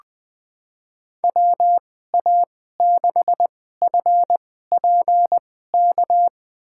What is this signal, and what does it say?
Morse code, heard as a keyed tone, about 20 wpm, spaced as for 14 wpm, 700 Hz. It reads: WA6FPK
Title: USA Call Signs - 6 Characters - 2x Character Spacing - 20wpm